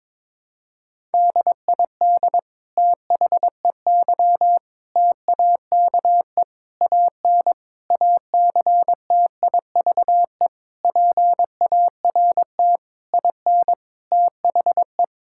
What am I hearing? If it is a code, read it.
Morse code, heard as a keyed tone, about 22 wpm, 700 Hz. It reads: DID THEY TAKE AN ACTIVE PART IN THE
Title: Sentences from Top 1000 Words - Review Part 6 - 22wpm